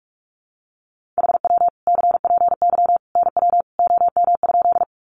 Morse code, heard as a keyed tone, about 44 wpm, 700 Hz. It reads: HW CPY NW OM?
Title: Common QSO Phrases 45wpm